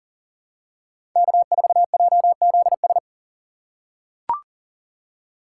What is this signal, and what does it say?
Morse code, heard as a keyed tone, about 37 wpm, 700 Hz. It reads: K4JZS
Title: USA Call Signs - 5 Characters - 3x Character Spacing - 40wpm